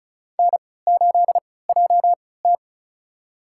Morse code, heard as a keyed tone, about 34 wpm, spaced as for 20 wpm, 700 Hz. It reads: N8JT
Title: USA Call Signs - 3x Character Spacing - 35wpm